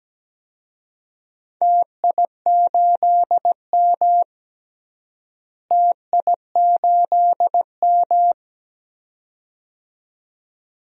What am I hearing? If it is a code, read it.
Morse code, heard as a keyed tone, about 17 wpm, 700 Hz. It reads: TI8M TI8M
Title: DXCC-100 Call Signs - 4 Characters - Repeated 2x - 17wpm